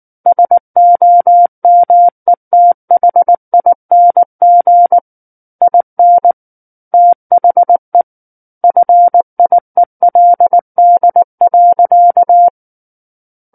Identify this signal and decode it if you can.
Morse code, heard as a keyed tone, about 19 wpm, 700 Hz. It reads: SOMETHING IN THE FIELD.
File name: Book One / 10-Chapter 10 - 19 WPM